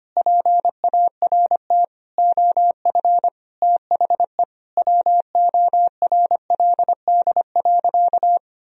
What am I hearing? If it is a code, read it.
Morse code, heard as a keyed tone, about 25 wpm, 700 Hz. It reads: PART OF THE WORLD.
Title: Sentences from Top 1000 Words - Review Part 1 - 25wpm